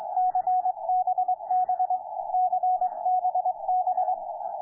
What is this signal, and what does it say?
Morse code, heard as a keyed tone, about 25 wpm, 740 Hz. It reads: CBB CBB
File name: ZB2FK-160-CW-19MAR15